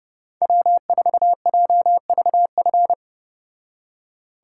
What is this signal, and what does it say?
Morse code, heard as a keyed tone, about 32 wpm, 700 Hz. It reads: W4JVF